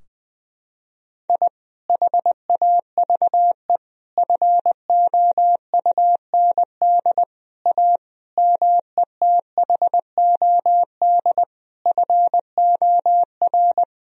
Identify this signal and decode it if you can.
Morse code, heard as a keyed tone, about 21 wpm, 700 Hz. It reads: I HAVE FOUND A METHOD FOR
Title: Sentences from Top 1000 Words - Review Part 3 - 20wpm